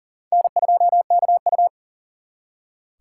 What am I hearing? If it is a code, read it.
Morse code, heard as a keyed tone, about 51 wpm, 700 Hz. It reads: N2KU